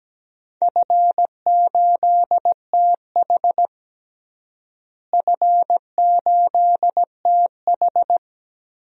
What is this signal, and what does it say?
Morse code, heard as a keyed tone, about 17 wpm, 700 Hz. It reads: F8TH F8TH
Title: DXCC-100 Call Signs - 4 Characters - Repeated 2x - 17wpm